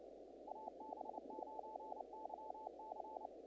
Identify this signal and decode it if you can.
Morse code, heard as a keyed tone, about 28 wpm, 870 Hz, with noise. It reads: AB9YC
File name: contests / naqp-cw-k5tr-op-k5pi- -1900